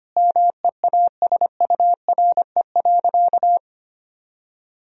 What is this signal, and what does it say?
Morse code, heard as a keyed tone, about 25 wpm, 700 Hz. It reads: MEASURE.
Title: Sentences from Top 1000 Words - Review Part 1 - 25wpm